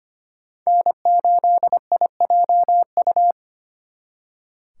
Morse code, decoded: N8IJU

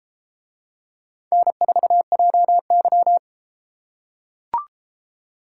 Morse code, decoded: N4JY

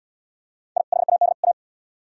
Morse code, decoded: I HAD A

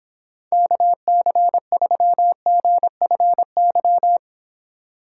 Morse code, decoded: KC3GFY

